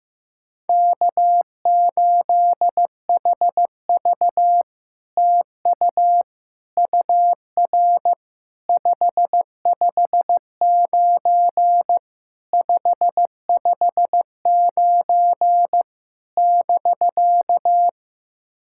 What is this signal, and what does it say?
Morse code, heard as a keyed tone, about 15 wpm, 700 Hz. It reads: K8HV TU UR 559 559 <BK>
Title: SOTA QSO Practice 15wpm